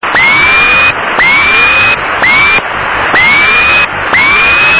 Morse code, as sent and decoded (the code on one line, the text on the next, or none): none